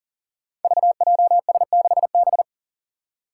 Morse code, heard as a keyed tone, about 40 wpm, 700 Hz. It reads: VJS6B
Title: Five Challenging Letter-Number Combinations 40wpm